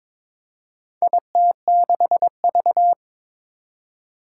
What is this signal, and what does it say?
Morse code, heard as a keyed tone, about 22 wpm, 700 Hz. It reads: IT6V
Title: Four Challenging Letter-Number Combinations 22wpm